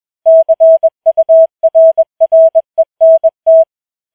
Morse code, heard as a keyed tone, about 21 wpm, 650 Hz. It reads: CURRENT
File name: challenge-184